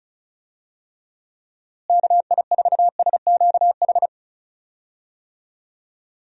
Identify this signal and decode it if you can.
Morse code, heard as a keyed tone, about 34 wpm, 700 Hz. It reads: KI4SQH
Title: USA Call Signs - 6 Characters - Repeated 3x - 35wpm